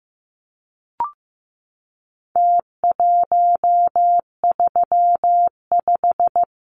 Morse code, heard as a keyed tone, about 15 wpm, 700 Hz. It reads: T135
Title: Numbers - Contest Style - 4-Digit 15wpm